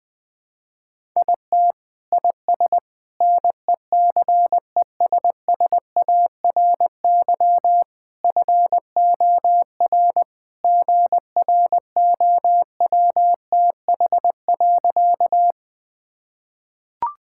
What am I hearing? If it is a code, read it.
Morse code, heard as a keyed tone, about 20 wpm, 700 Hz. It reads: IT IS NECESSARY FOR GROWTH.
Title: Sentences from Top 1000 Words - Review Part 5 - 20wpm